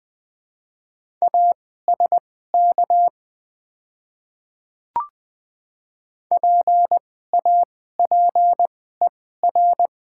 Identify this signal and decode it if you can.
Morse code, heard as a keyed tone, about 20 wpm, spaced as for 15 wpm, 700 Hz. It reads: ASK PAPER
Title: 2x Character Spacing - Top 300 Words - 20wpm